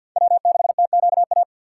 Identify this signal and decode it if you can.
Morse code, heard as a keyed tone, about 50 wpm, 700 Hz. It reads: W6TQA
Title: USA Call Signs - 5 Characters - 2x Character Spacing - 50wpm